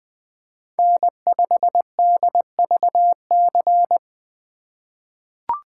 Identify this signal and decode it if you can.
Morse code, heard as a keyed tone, about 20 wpm, 700 Hz. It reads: N5DVC
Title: USA Call Signs - 5 Characters - 2x Character Spacing - 20wpm